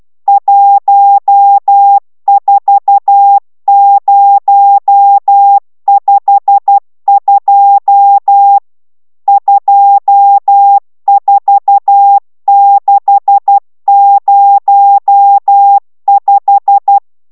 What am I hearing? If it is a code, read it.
Morse code, heard as a keyed tone, about 12 wpm, 800 Hz. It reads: 14052 24605